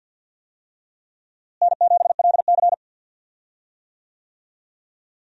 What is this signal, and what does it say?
Morse code, heard as a keyed tone, about 52 wpm, 700 Hz. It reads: N7LC
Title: USA Call Signs - 4 Characters - Repeated 2x - 50wpm